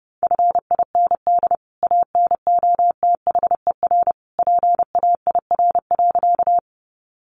Morse code, heard as a keyed tone, about 30 wpm, 700 Hz. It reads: FIND ANOTHER PAIR.
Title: Sentences from Top 1200 Words 30wpm